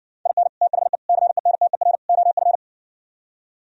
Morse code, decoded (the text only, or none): IN THE CENTER OF